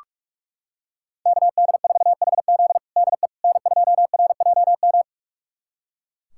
Morse code, heard as a keyed tone, about 43 wpm, 700 Hz. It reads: KD4HZ DE N2RJM
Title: USA Callsign DE Callsign - 5 Characters - 45wpm